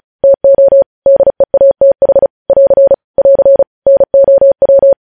Morse code, heard as a keyed tone, about 36 wpm, 550 Hz. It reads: TO DEATH <AR> <AR> NOW
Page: news headlines in morse code at 35 wpm